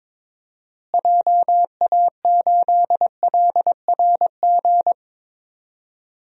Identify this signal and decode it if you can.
Morse code, heard as a keyed tone, about 22 wpm, 700 Hz. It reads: JA8LRG
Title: DXCC-100 Call Signs - 6 Characters - Repeated 3x - 22wpm